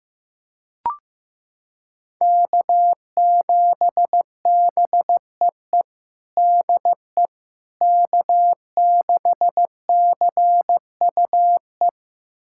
Morse code, heard as a keyed tone, about 16 wpm, 700 Hz. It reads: K7BEE DE K6CUE
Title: USA Callsign DE Callsign - 5 Characters - 15wpm